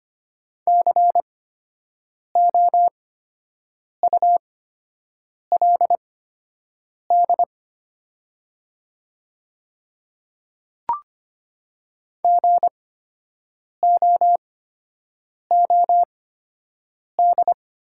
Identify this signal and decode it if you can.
Morse code, heard as a keyed tone, about 25 wpm, spaced as for 7 wpm, 700 Hz. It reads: COULD GOOD
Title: 8x Character Spacing - Top 300 Words - 25wpm